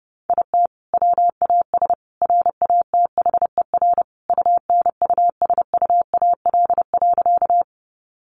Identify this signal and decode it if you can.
Morse code, heard as a keyed tone, about 30 wpm, 700 Hz. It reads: IT WAS RATHER UNUSUAL.
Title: Sentences from Top 1400 Words 30wpm